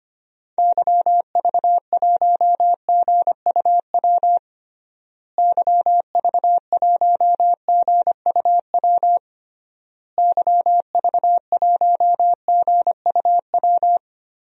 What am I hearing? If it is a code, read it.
Morse code, heard as a keyed tone, about 25 wpm, 700 Hz. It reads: YV1GUW YV1GUW YV1GUW
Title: DXCC-100 Call Signs - 6 Characters - Repeated 3x - 25wpm